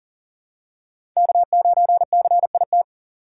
Morse code, heard as a keyed tone, about 39 wpm, 700 Hz. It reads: K9CIT
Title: USA Call Signs - 5 Characters - 3x Character Spacing - 40wpm